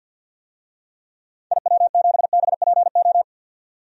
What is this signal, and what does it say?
Morse code, heard as a keyed tone, about 53 wpm, 700 Hz. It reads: IW7BPO
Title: DXCC-100 Call Signs - 6 Characters - 50wpm